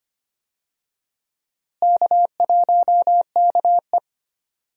Morse code, heard as a keyed tone, about 25 wpm, 700 Hz. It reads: K1KE
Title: USA Call Signs - 4 Characters - Repeated 3x - 25wpm